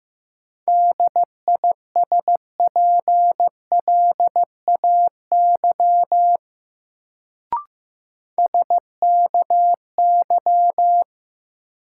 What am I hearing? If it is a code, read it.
Morse code, heard as a keyed tone, about 16 wpm, 700 Hz. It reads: DISPLAY SKY
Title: Sentences from Top 1200 Words - New Word Drill - 15wpm